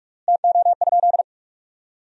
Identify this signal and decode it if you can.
Morse code, heard as a keyed tone, about 46 wpm, 700 Hz. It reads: TO?
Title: 3x Word-Spacing - Sentences from Top 500 Words - 45wpm